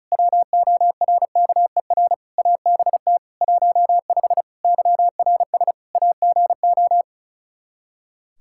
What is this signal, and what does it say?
Morse code, heard as a keyed tone, about 34 wpm, 700 Hz. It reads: WORKER ABT 15 YRS AGO
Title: Practice QSO - Exchange 3 - More Info - 35wpm